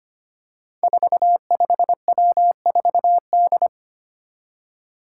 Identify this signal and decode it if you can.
Morse code, heard as a keyed tone, about 25 wpm, 700 Hz. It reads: 45W4D
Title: Five Challenging Letter-Number Combinations 25wpm